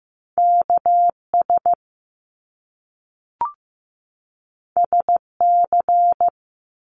Morse code, heard as a keyed tone, about 15 wpm, 700 Hz. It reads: KS SC
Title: US States 15wpm